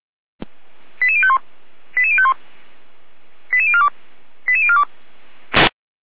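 Five-tone selective call (ZVEI), heard as quick runs of five tones, tones of about 70 ms. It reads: 88051 88051 88042 88042
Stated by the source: ZVEI five-tone selective call received on an SDR.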